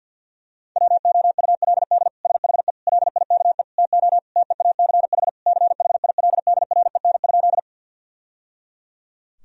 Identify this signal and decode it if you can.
Morse code, heard as a keyed tone, about 50 wpm, 700 Hz. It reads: WOULD SHE LIKE TO TEACH CHILDREN?
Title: Sentences from Top 1000 Words - Review Part 2 - 50wpm